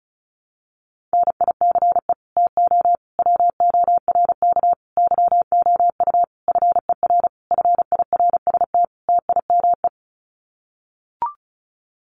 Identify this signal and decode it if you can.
Morse code, heard as a keyed tone, about 34 wpm, 700 Hz. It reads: NICE TO WORK YOU FER FIRST TIME
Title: Common QSO Phrases 35wpm